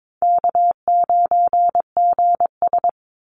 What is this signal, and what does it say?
Morse code, heard as a keyed tone, about 22 wpm, 700 Hz. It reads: K9GS
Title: Call Signs Repeated 3x - 22wpm